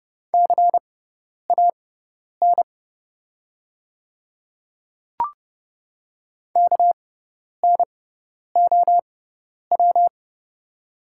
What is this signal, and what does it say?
Morse code, heard as a keyed tone, about 30 wpm, spaced as for 10 wpm, 700 Hz. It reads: CAN KNOW
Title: 6x Character Spacing - Top 100 Words - 30wpm